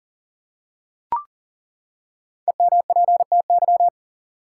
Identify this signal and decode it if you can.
Morse code, heard as a keyed tone, about 39 wpm, 700 Hz. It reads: EMPTY
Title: Five Letter Words - 40wpm